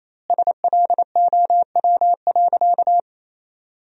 Morse code, heard as a keyed tone, about 28 wpm, 700 Hz. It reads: SLOW.
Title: 2x Word-Spacing - Sentences from Top 400 Words - 28wpm